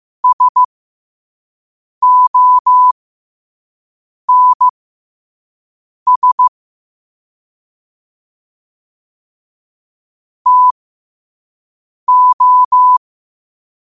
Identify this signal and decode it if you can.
Morse code, heard as a keyed tone, about 15 wpm, spaced as for 5 wpm, 1000 Hz. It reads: SONS TO
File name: qotdfiles / 5wpm